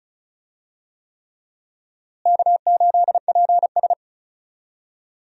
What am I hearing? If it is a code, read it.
Morse code, heard as a keyed tone, about 36 wpm, 700 Hz. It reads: K8PS
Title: USA Call Signs - 4 Characters - Repeated 3x - 35wpm